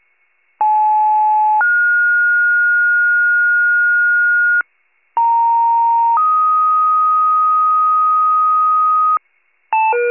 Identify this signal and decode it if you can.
Two-tone page, heard as a one-second tone, then a three-second tone, about 860 then 1430 Hz, about 930 then 1250 Hz, with noise. It is heard from 0.6 to 9.2 s.